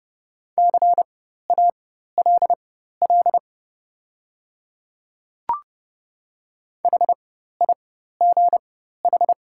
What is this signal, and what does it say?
Morse code, heard as a keyed tone, about 30 wpm, spaced as for 14 wpm, 700 Hz. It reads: CALL HIGH
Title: 4x Character Spacing - Top 300 Words - 30wpm